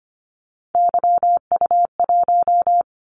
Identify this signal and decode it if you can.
Morse code, heard as a keyed tone, about 25 wpm, 700 Hz. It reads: YU1
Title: Callsign Prefixes for Top 25 DXCC - 25wpm